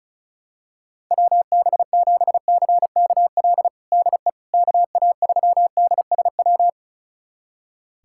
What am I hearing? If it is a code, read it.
Morse code, heard as a keyed tone, about 36 wpm, 700 Hz. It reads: WB7CKL DE KA3DSW
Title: USA Callsign DE Callsign - 6 Characters - 35wpm